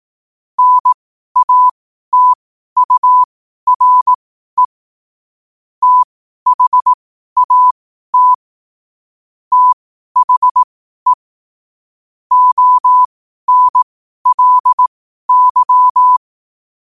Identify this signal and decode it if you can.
Morse code, heard as a keyed tone, about 18 wpm, spaced as for 12 wpm, 1000 Hz. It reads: NATURE THAT THE ONLY
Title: 12 WPM morse code quotes for Fri, 01 Mar 2024